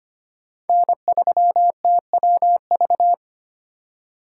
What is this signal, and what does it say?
Morse code, heard as a keyed tone, about 25 wpm, 700 Hz. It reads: N3TWV
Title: USA Call Signs - 5 Characters - 2x Character Spacing - 25wpm